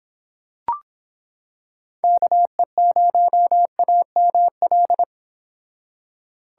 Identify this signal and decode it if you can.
Morse code, heard as a keyed tone, about 26 wpm, 700 Hz. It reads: KE0AML